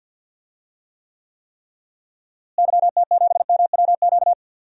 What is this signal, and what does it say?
Morse code, heard as a keyed tone, about 48 wpm, 700 Hz. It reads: YT7MWQ
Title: DXCC-100 Call Signs - 6 Characters - Repeated 3x - 50wpm